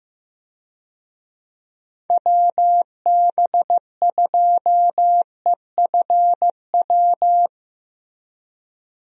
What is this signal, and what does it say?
Morse code, heard as a keyed tone, about 15 wpm, 700 Hz. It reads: WB2EFW
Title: USA Call Signs - 6 Characters - Repeated 3x - 15wpm